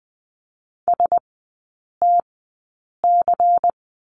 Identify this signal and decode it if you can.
Morse code, heard as a keyed tone, about 20 wpm, spaced as for 8 wpm, 690 Hz. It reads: STC
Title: 093 Lesson 16e - Letter c - Three Characters 20wpm